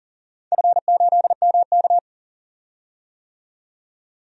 Morse code, heard as a keyed tone, about 40 wpm, 700 Hz. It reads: F8MK
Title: DXCC-100 Call Signs - 4 Characters - 40wpm